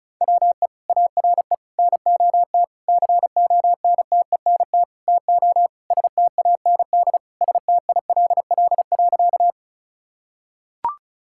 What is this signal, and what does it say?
Morse code, heard as a keyed tone, about 36 wpm, 700 Hz. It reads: WE ARE NOT CONTENT TO STAND STILL.
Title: Sentences from Top 1000 Words - Review Part 6 - 35wpm